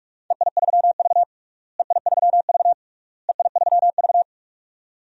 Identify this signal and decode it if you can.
Morse code, heard as a keyed tone, about 46 wpm, 700 Hz. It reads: EI3V EI3V EI3V
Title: DXCC-100 Call Signs - 4 Characters - Repeated 3x - 45wpm